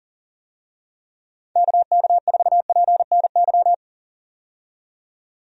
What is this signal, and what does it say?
Morse code, heard as a keyed tone, about 40 wpm, 700 Hz. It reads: KK4PNY
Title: USA Call Signs - 6 Characters - 40wpm